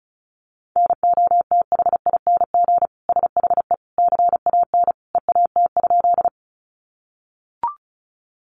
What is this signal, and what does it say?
Morse code, heard as a keyed tone, about 34 wpm, 700 Hz. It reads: NOTHING SHE CAN EAT?
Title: Sentences from Top 500 Words 35wpm